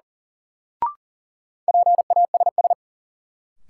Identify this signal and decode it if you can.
Morse code, heard as a keyed tone, about 37 wpm, 700 Hz. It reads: PASS
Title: Sentences from Top 400 Words - New Word Drill - 40wpm